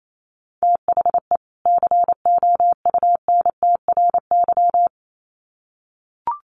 Morse code, heard as a keyed tone, about 28 wpm, 700 Hz. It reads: THE COUNTRY
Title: Sets of 2 Words 28wpm